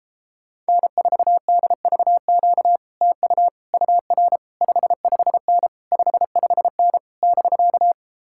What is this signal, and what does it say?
Morse code, heard as a keyed tone, about 34 wpm, 700 Hz. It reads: N4DVQ TU UR 55N 55N <BK>